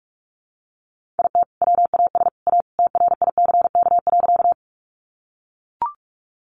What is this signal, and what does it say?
Morse code, heard as a keyed tone, about 44 wpm, 700 Hz. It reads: IT WAS A TRICK.
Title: Sentences from Top 1500 Words 45wpm